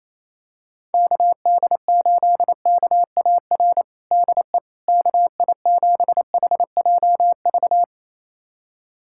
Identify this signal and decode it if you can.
Morse code, heard as a keyed tone, about 28 wpm, 700 Hz. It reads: KD8KAR DE KI7HJV
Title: USA Callsign DE Callsign - 6 Characters - 28wpm